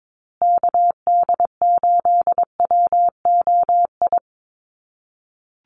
Morse code, heard as a keyed tone, about 22 wpm, 700 Hz. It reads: KD8WOI
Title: USA Call Signs Repeated 2x 22wpm